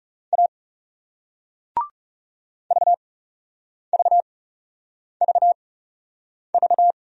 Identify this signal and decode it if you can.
Morse code, heard as a keyed tone, about 38 wpm, spaced as for 8 wpm, 700 Hz. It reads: A VVVV